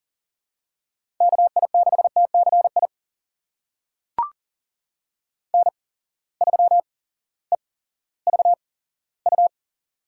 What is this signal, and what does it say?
Morse code, heard as a keyed tone, about 40 wpm, spaced as for 18 wpm, 700 Hz. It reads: KI6TCI N3EVU
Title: USA Call Signs - 8x Character Spacing - 40wpm